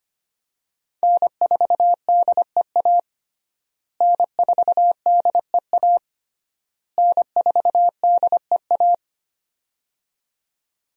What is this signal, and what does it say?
Morse code, heard as a keyed tone, about 25 wpm, 700 Hz. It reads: N4DEA N4DEA N4DEA
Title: USA Call Signs - 5 Characters - Repeated 3x - 25wpm